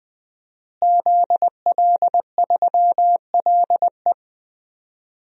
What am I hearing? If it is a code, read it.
Morse code, heard as a keyed tone, about 20 wpm, 700 Hz. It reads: ZL3LE
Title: DXCC-100 Call Signs - 5 Characters - Repeated 2x - 20wpm